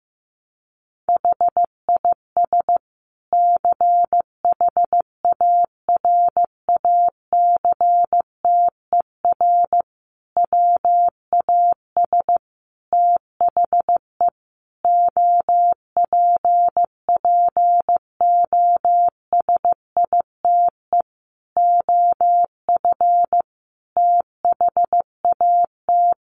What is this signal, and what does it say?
Morse code, heard as a keyed tone, about 15 wpm, 700 Hz. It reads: HIS CHARACTER WAS THE OPPOSITE OF THAT
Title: Sentences from Top 1300 Words 15wpm